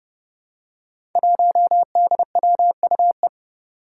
Morse code, heard as a keyed tone, about 30 wpm, 700 Hz. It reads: 1DWUE
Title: Five Challenging Letter-Number Combinations 30wpm